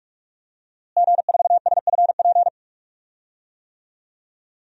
Morse code, heard as a keyed tone, about 46 wpm, 700 Hz. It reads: G4SFP
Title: DXCC-100 Call Signs - 5 Characters - Repeated 2x - 45wpm